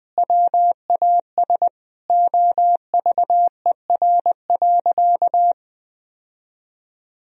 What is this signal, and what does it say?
Morse code, heard as a keyed tone, about 20 wpm, 700 Hz. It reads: WAS OVER.
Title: Sentences from Top 1000 Words - Review Part 4 - 20wpm